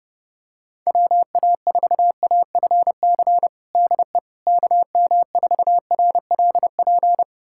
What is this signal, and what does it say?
Morse code, heard as a keyed tone, about 30 wpm, 700 Hz. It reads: WA4AFC DE KM4RLP
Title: USA Callsign DE Callsign - 6 Characters - 30wpm